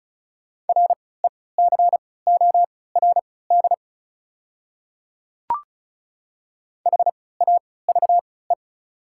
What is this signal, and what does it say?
Morse code, heard as a keyed tone, about 36 wpm, spaced as for 20 wpm, 690 Hz. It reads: RECORD HAVE
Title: 3x Character Spacing - Top 500 Words - 35wpm